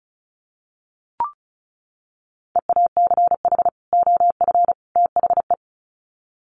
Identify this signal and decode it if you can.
Morse code, heard as a keyed tone, about 34 wpm, 700 Hz. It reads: EACH OF THE
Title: Sets of 3 Words 35wpm